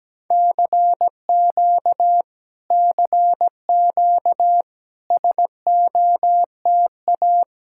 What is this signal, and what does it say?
Morse code, heard as a keyed tone, about 17 wpm, 700 Hz. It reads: CQ CQ SOTA
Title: Speed-Racing Common QSO Phrases Encore - 17wpm